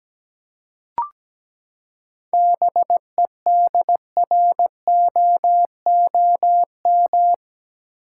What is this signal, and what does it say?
Morse code, heard as a keyed tone, about 17 wpm, 700 Hz. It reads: BEDROOM
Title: Seven Letter Words - 17wpm